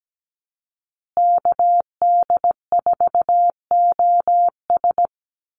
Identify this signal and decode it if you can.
Morse code, heard as a keyed tone, about 17 wpm, 700 Hz. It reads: KD4OS
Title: Call Signs Repeated 3x - 17wpm